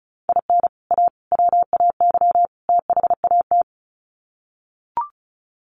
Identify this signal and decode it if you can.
Morse code, heard as a keyed tone, about 34 wpm, 700 Hz. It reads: IN A WAY THAT
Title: Sets of 4 Words 35wpm